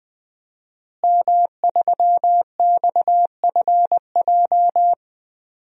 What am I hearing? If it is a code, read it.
Morse code, heard as a keyed tone, about 20 wpm, 700 Hz. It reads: M3XFJ
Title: DXCC-100 Call Signs - 5 Characters - Repeated 2x - 20wpm